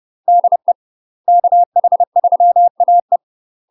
Morse code, heard as a keyed tone, about 30 wpm, 700 Hz. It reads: DE KH3AE